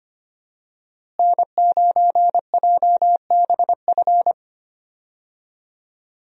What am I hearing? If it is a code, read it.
Morse code, heard as a keyed tone, about 25 wpm, 700 Hz. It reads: N9JBF
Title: USA Call Signs - 5 Characters - Repeated 2x - 25wpm